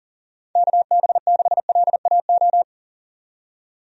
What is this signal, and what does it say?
Morse code, heard as a keyed tone, about 40 wpm, 700 Hz. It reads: KB6LAO
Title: USA Call Signs - 6 Characters - 40wpm